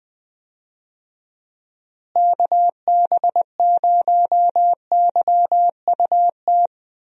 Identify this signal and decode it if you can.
Morse code, heard as a keyed tone, about 21 wpm, 700 Hz. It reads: KB0YUT
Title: USA Call Signs - 6 Characters - Repeated 2x - 20wpm